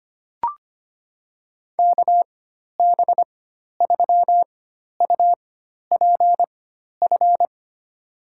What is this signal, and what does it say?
Morse code, heard as a keyed tone, about 25 wpm, spaced as for 12 wpm, 700 Hz. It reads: KB3UPF
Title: USA Call Signs - 6 Characters - 4x Character Spacing - 25wpm